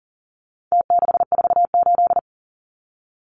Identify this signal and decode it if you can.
Morse code, heard as a keyed tone, about 40 wpm, 700 Hz. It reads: T648
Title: Numbers - Contest Style - 4-Digit 40wpm